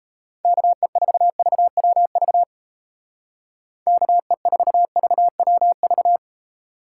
Morse code, heard as a keyed tone, about 37 wpm, 700 Hz. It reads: KE4VWV KE4VWV